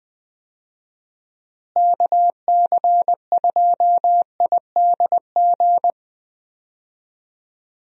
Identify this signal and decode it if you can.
Morse code, heard as a keyed tone, about 21 wpm, 700 Hz. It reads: KC2IDG
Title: USA Call Signs - 6 Characters - Repeated 2x - 20wpm